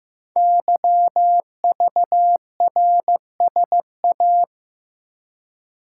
Morse code, heard as a keyed tone, about 15 wpm, 700 Hz. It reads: YVRSA
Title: Code Groups - 5 Letters - 15wpm